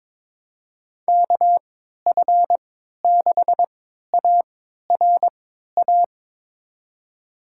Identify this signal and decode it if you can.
Morse code, heard as a keyed tone, about 22 wpm, spaced as for 12 wpm, 700 Hz. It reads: KF6ARA
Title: USA Call Signs - 6 Characters - 3x Character Spacing - 22wpm